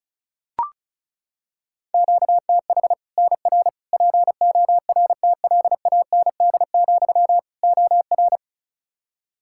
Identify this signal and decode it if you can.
Morse code, heard as a keyed tone, about 36 wpm, 700 Hz. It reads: QTH NR PORTLAND, OR
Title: Practice QSO - Exchange 1 - Beginning - 35wpm